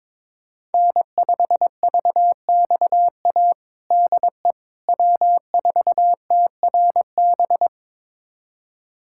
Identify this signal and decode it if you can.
Morse code, heard as a keyed tone, about 22 wpm, 700 Hz. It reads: N5VXA DE W4TRB
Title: USA Callsign DE Callsign - 5 Characters - 22wpm